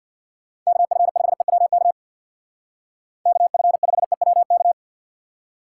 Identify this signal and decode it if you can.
Morse code, heard as a keyed tone, about 59 wpm, 700 Hz. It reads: KF5EWK KF5EWK
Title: Speed Racing - USA 2x3 Call Signs 50wpm